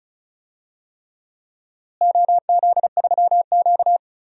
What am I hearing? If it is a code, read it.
Morse code, heard as a keyed tone, about 36 wpm, 700 Hz. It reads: OZ3Q
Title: DXCC-100 Call Signs - 4 Characters - Repeated 3x - 35wpm